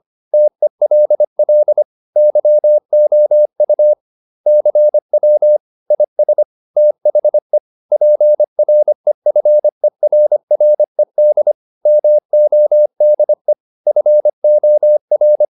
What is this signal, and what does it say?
Morse code, heard as a keyed tone, about 25 wpm, 600 Hz. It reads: TELL YOU CW IS THE PREFERRED MODE FOR